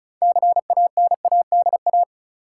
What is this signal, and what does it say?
Morse code, heard as a keyed tone, about 36 wpm, 700 Hz. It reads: CANADA
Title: Sets of 5 Words - New Word Drill - 35wpm